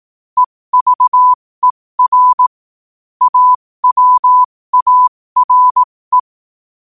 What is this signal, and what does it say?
Morse code, heard as a keyed tone, about 18 wpm, spaced as for 15 wpm, 1000 Hz. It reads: EVER AWARE
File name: qotdfiles / 15wpm